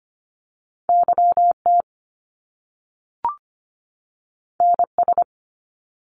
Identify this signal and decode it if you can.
Morse code, heard as a keyed tone, about 25 wpm, 700 Hz. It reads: YT NS
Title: Canadian Provinces and Territories 25wpm